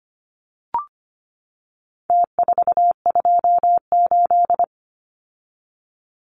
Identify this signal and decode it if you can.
Morse code, heard as a keyed tone, about 25 wpm, 700 Hz. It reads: T428
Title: Numbers - Contest Style - 4-Digit 25wpm